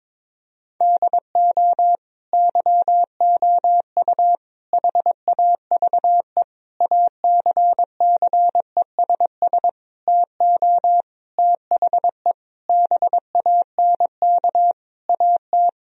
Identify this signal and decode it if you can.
Morse code, heard as a keyed tone, about 22 wpm, 700 Hz. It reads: DO YOU HAVE ACCESS TO THE BANK AT
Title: Sentences from Top 1000 Words - Review Part 4 - 22wpm